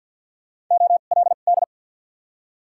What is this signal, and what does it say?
Morse code, heard as a keyed tone, about 53 wpm, spaced as for 37 wpm, 700 Hz. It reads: OLD
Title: 2x Character Spacing - Top 500 Words - 50wpm